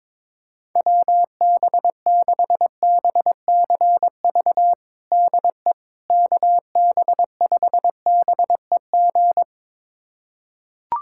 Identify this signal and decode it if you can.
Morse code, heard as a keyed tone, about 21 wpm, 700 Hz. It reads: WB6BCV DE KB5BEG
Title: USA Callsign DE Callsign - 6 Characters - 22wpm